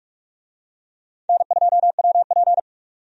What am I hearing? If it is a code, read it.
Morse code, heard as a keyed tone, about 46 wpm, 700 Hz. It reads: N2WP
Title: USA Call Signs - 4 Characters - 45wpm